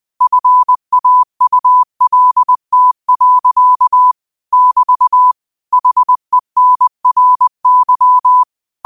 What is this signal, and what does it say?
Morse code, heard as a keyed tone, about 20 wpm, 1000 Hz. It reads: FAULT. <BT> HENRY